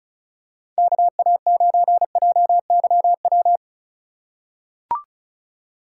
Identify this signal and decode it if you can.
Morse code, heard as a keyed tone, about 36 wpm, 690 Hz. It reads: KA9JYW
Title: USA Call Signs - 6 Characters - 6x Character Spacing - 35wpm